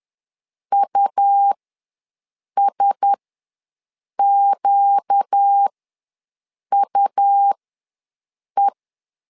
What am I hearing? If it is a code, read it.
Morse code, heard as a keyed tone, about 10 wpm, spaced as for 6 wpm, 800 Hz. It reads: USQUE
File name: LEZIONI / 4 Q1S ws 10wpm 2208